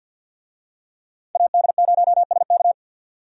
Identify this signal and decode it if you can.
Morse code, heard as a keyed tone, about 49 wpm, 700 Hz. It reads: AD0SK
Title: USA Call Signs - 5 Characters - 50wpm